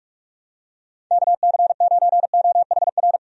Morse code, heard as a keyed tone, about 46 wpm, 700 Hz. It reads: KC9OHR
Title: USA Call Signs - 6 Characters - 45wpm